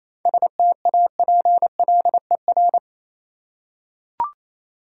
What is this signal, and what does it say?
Morse code, heard as a keyed tone, about 28 wpm, 690 Hz. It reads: STAPLER